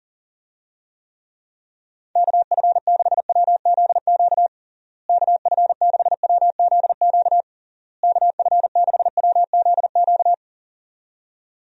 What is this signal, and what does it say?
Morse code, heard as a keyed tone, about 40 wpm, 700 Hz. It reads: KF6WZQ KF6WZQ KF6WZQ
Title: USA Call Signs - 6 Characters - Repeated 3x - 40wpm